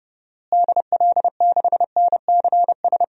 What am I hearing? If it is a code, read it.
Morse code, heard as a keyed tone, about 30 wpm, 700 Hz. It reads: DL6NCS
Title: DXCC-100 Call Signs - 6 Characters - Repeated 2x - 30wpm